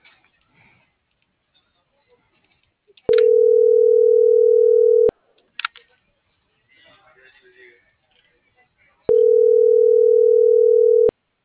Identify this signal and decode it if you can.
Ringback tone (North America), 440 + 480 Hz.